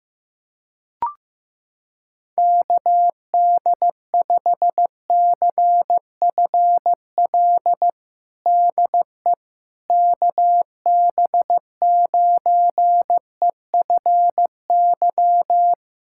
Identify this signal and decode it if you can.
Morse code, heard as a keyed tone, about 15 wpm, 700 Hz. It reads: KD5CFL DE KB9EFY
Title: USA Callsign DE Callsign 15wpm